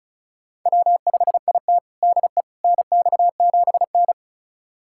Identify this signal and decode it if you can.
Morse code, heard as a keyed tone, about 36 wpm, 700 Hz. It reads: W5IT DE NX7N
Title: USA Callsign DE Callsign - 4 Characters - 35wpm